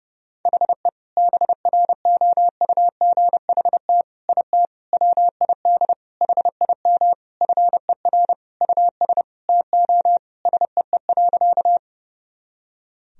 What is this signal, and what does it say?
Morse code, heard as a keyed tone, about 30 wpm, 700 Hz. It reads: HE BROUGHT IT WID HIM FER US TO SEE.
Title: Sentences from Top 500 Words and Abbreviations 30wpm